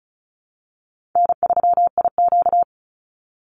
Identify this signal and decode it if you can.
Morse code, heard as a keyed tone, about 34 wpm, 700 Hz. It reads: N3IQ
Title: Call Signs Repeated 2x - 35wpm